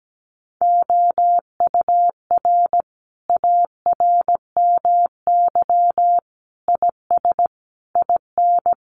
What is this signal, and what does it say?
Morse code, heard as a keyed tone, about 17 wpm, 700 Hz. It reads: OUR ARMY IS IN
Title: Sentences from Top 1500 Words 17wpm